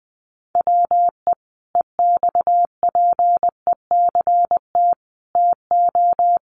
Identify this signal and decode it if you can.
Morse code, heard as a keyed tone, about 19 wpm, 700 Hz. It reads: WE EXPECT TO
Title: Sentences from Top 1100 Words 20wpm